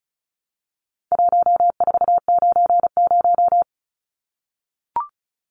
Morse code, heard as a keyed tone, about 34 wpm, 700 Hz. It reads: 1490
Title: Numbers - 4-Digit 35wpm